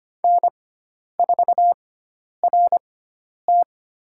Morse code, heard as a keyed tone, about 25 wpm, spaced as for 10 wpm, 700 Hz. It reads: N4RT
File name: Lesson 16g - Letter c - Call Signs 25wpm